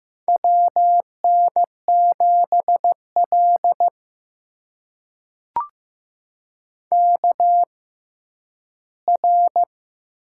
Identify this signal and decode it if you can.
Morse code, heard as a keyed tone, about 15 wpm, spaced as for 10 wpm, 700 Hz. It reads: WN7L KR
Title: USA Call Signs - 4 Characters - 6x Character Spacing - 15wpm